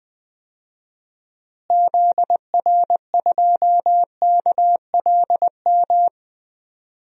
Morse code, heard as a keyed tone, about 21 wpm, 700 Hz. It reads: ZR2KLM
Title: DXCC-100 Call Signs - 6 Characters - Repeated 2x - 20wpm